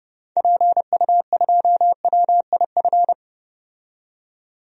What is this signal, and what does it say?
Morse code, heard as a keyed tone, about 30 wpm, 690 Hz. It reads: PU2WIF
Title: DXCC-100 Call Signs - 6 Characters - 30wpm